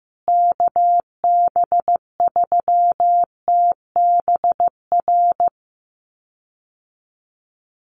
Morse code, heard as a keyed tone, about 14 wpm, 700 Hz. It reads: KB3TBR
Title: USA Call Signs Repeated 3x 15wpm